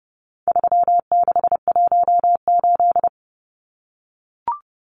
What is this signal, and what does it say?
Morse code, heard as a keyed tone, about 30 wpm, 700 Hz. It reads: 3618
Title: Numbers - 4-Digit 30wpm